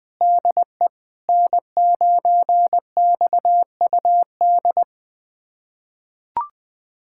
Morse code, heard as a keyed tone, about 21 wpm, 700 Hz. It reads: DE N9XUD
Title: USA Callsign DE Callsign - 5 Characters - 20wpm